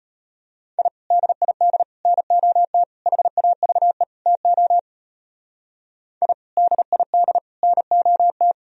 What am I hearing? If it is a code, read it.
Morse code, heard as a keyed tone, about 33 wpm, 700 Hz. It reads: I DID NOT HAVE TO I DID NOT